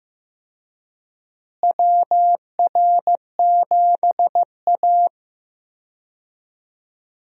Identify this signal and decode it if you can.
Morse code, heard as a keyed tone, about 15 wpm, 700 Hz. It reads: WR7A
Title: USA Call Signs - 4 Characters - Repeated 2x - 15wpm